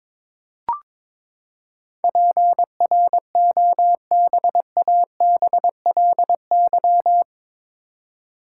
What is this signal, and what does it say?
Morse code, heard as a keyed tone, about 22 wpm, 700 Hz. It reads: PROBABLY
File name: practice / 22wpm / Eight Letter Words - 22wpm